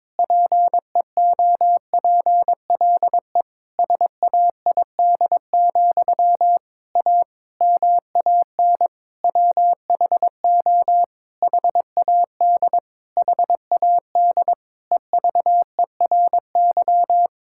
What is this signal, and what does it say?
Morse code, heard as a keyed tone, about 22 wpm, 700 Hz. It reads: PEOPLE SAID, A MAN WHO HAD HAD EVERY
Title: Sentences from Top 1000 Words - Review Part 6 - 22wpm